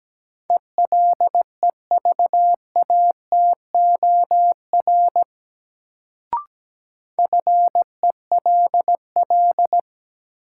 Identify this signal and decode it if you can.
Morse code, heard as a keyed tone, about 17 wpm, 700 Hz. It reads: ELEVATOR FELL
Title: Sentences from Top 1600 Words - New Word Drill - 17wpm